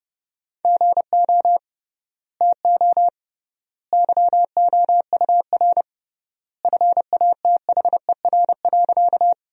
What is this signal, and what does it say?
Morse code, heard as a keyed tone, about 30 wpm, 700 Hz. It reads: GO TO YOUR FATHER.
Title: 3x Word-Spacing - Sentences from Top 300 Words - 30wpm